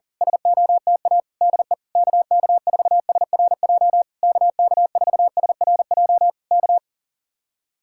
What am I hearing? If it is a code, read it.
Morse code, heard as a keyed tone, about 38 wpm, 700 Hz. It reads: SOTA DE KK4SRJ KK4SRJ K
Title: Speed-Racing Common QSO Phrases Encore - 40wpm